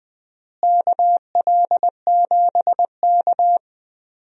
Morse code, heard as a keyed tone, about 20 wpm, 700 Hz. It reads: KL7K
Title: DXCC-100 Call Signs - 4 Characters - Repeated 2x - 20wpm